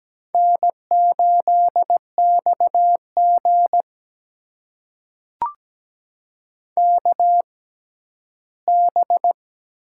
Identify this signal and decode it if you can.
Morse code, heard as a keyed tone, about 17 wpm, spaced as for 12 wpm, 700 Hz. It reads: N8XG KB
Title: USA Call Signs - 4 Characters - 6x Character Spacing - 17wpm